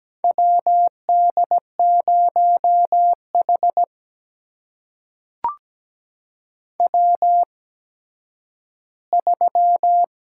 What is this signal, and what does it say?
Morse code, heard as a keyed tone, about 17 wpm, spaced as for 10 wpm, 700 Hz. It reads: WD0H W3
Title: USA Call Signs - 4 Characters - 8x Character Spacing - 17wpm